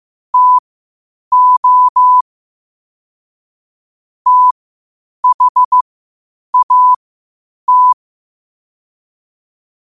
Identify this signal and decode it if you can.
Morse code, heard as a keyed tone, about 15 wpm, spaced as for 8 wpm, 1000 Hz. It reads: TO THAT
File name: qotdfiles / 8wpm